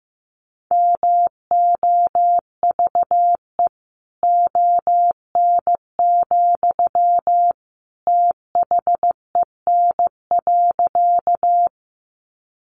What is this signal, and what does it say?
Morse code, heard as a keyed tone, about 15 wpm, 700 Hz. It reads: MOVE ON, THEN.
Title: Sentences from Top 200 Words 15wpm